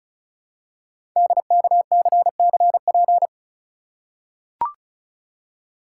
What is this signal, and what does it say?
Morse code, heard as a keyed tone, about 36 wpm, 700 Hz. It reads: DKCCP
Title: Code Groups - 5 Letters - 35wpm